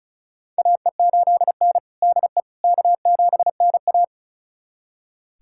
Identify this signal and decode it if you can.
Morse code, heard as a keyed tone, about 36 wpm, 700 Hz. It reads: AE8N DE K7NA
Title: USA Callsign DE Callsign - 4 Characters - 35wpm